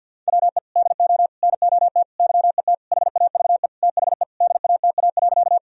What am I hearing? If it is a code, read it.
Morse code, heard as a keyed tone, about 50 wpm, 700 Hz. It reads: WE DO NOT YET HAVE THE DATA.
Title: Sentences from Top 1000 Words - Review Part 3 - 50wpm